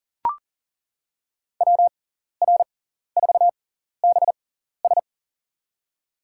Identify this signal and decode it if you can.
Morse code, heard as a keyed tone, about 40 wpm, spaced as for 14 wpm, 700 Hz. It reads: WR4BS
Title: USA Call Signs - 5 Characters - 6x Character Spacing - 40wpm